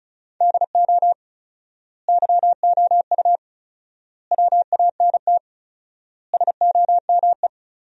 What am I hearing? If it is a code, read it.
Morse code, heard as a keyed tone, about 34 wpm, 700 Hz. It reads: DO YOU WANT SOME
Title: 4x Word-Spacing - Sentences from Top 300 Words - 35wpm